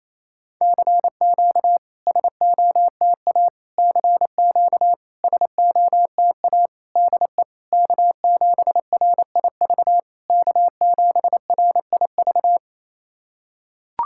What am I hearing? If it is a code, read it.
Morse code, heard as a keyed tone, about 28 wpm, 690 Hz. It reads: CQ SOTA CQ SOTA DE K7RIV K7RIV
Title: SOTA QSO Practice 28wpm